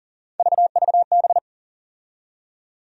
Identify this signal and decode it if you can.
Morse code, heard as a keyed tone, about 40 wpm, 700 Hz. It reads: VVB
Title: USA Call Sign Postfixes - 3 Letters 40wpm